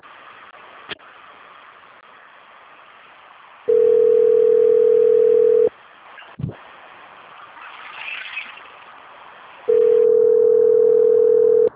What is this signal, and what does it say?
Ringback tone (North America), 440 + 480 Hz.
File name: out- - - - - .43351